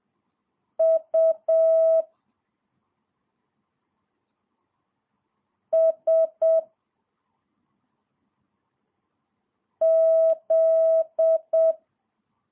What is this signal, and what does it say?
Morse code, heard as a keyed tone, about 7 wpm, 650 Hz. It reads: U S Z
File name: jh8cbh / 2022cw